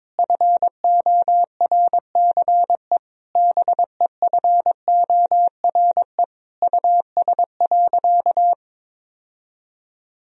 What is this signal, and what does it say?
Morse code, heard as a keyed tone, about 22 wpm, 700 Hz. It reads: FORCE BEFORE US.
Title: Sentences from Top 1000 Words - Review Part 3 - 22wpm